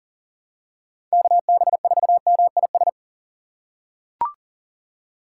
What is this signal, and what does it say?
Morse code, heard as a keyed tone, about 40 wpm, 700 Hz. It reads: KB4MIS
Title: USA Call Signs - 6 Characters - 6x Character Spacing - 40wpm